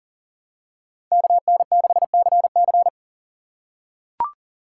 Morse code, heard as a keyed tone, about 40 wpm, 700 Hz. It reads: KN6CC
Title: USA Call Signs - 5 Characters - 4x Character Spacing - 40wpm